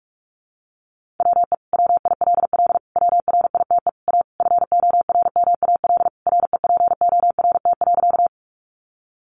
Morse code, heard as a keyed tone, about 44 wpm, 700 Hz. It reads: WE WILL WRITE A FORMAL REPORT.
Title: Sentences from Top 1300 Words 45wpm